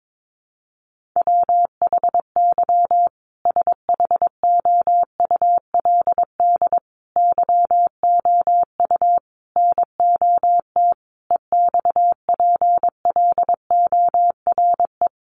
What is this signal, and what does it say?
Morse code, heard as a keyed tone, about 22 wpm, 700 Hz. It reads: WHY SHOULD YOU NOT EXPLORE
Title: Sentences from Top 1100 Words 22wpm